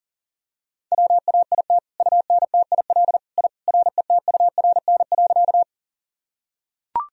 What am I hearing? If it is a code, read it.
Morse code, heard as a keyed tone, about 37 wpm, 700 Hz. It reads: WAIT UNTIL I RETURN.
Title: Sentences from Top 1000 Words - Review Part 1 - 40wpm